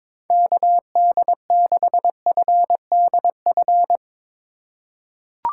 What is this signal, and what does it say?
Morse code, heard as a keyed tone, about 22 wpm, 700 Hz. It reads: KD6FDF
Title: USA Call Signs - 4x Character Spacing - 22wpm